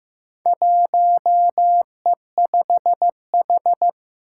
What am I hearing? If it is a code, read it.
Morse code, heard as a keyed tone, about 15 wpm, 700 Hz. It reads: 1E5H
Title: Four Challenging Letter-Number Combinations 15wpm